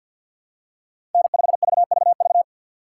Morse code, heard as a keyed tone, about 51 wpm, 690 Hz. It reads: N5VVV
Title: USA Call Signs - 5 Characters - 3x Character Spacing - 50wpm